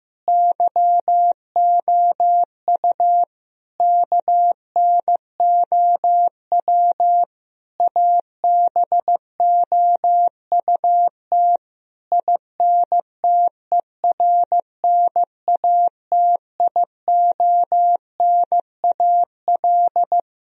Morse code, heard as a keyed tone, about 15 wpm, 700 Hz. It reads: YOU KNOW ABOUT INTERNATIONAL
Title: Sentences from Top 1000 Words - Review Part 4 - 15wpm